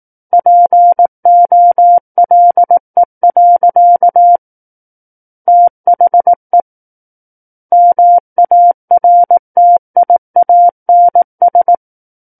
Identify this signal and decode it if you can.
Morse code, heard as a keyed tone, about 18 wpm, 700 Hz. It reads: POLE. THE MARTIANS
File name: Book One / 09-Chapter 9 - 18 WPM